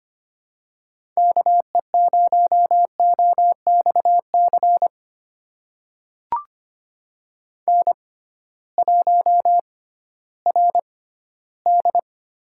Morse code, heard as a keyed tone, about 25 wpm, spaced as for 15 wpm, 700 Hz. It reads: KE0OXC N1RD
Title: USA Call Signs - 6x Character Spacing - 25wpm